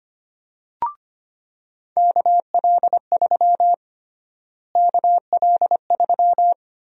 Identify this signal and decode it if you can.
Morse code, heard as a keyed tone, about 25 wpm, 700 Hz. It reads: KL3 KL3
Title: USA Call Sign Prefixes - Repeated 2x - 25wpm